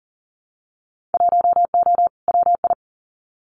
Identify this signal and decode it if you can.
Morse code, heard as a keyed tone, about 40 wpm, 700 Hz. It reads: 1O WI
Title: Winter Field Day Exchanges 40wpm